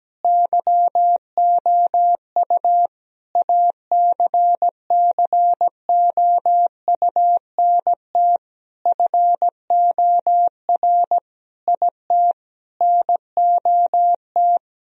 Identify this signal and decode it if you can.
Morse code, heard as a keyed tone, about 17 wpm, 700 Hz. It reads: YOU ACCOUNT FOR IT NOT
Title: Sentences from Top 1000 Words - Review Part 3 - 17wpm